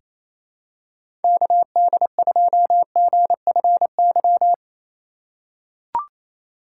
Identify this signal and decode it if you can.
Morse code, heard as a keyed tone, about 27 wpm, 700 Hz. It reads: KD2GFY
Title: USA Call Signs - 6 Characters - 3x Character Spacing - 28wpm